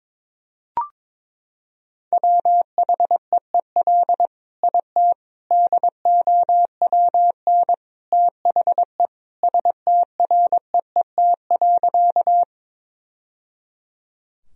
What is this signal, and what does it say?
Morse code, heard as a keyed tone, about 21 wpm, 700 Hz. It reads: WHEEL IT DOWN THE STREET.
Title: Sentences from Top 1000 Words - Review Part 3 - 22wpm